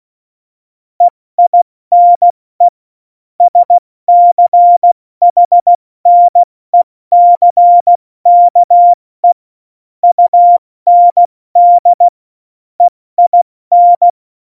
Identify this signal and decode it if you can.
Morse code, heard as a keyed tone, about 16 wpm, 700 Hz. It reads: EINE SCHNECKE UND EIN